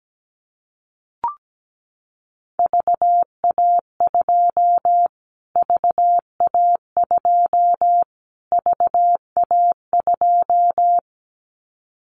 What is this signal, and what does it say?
Morse code, heard as a keyed tone, about 17 wpm, 700 Hz. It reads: VA2 VA2 VA2
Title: Callsign Prefixes for Top 25 DXCC - Repeated 3x - 17wpm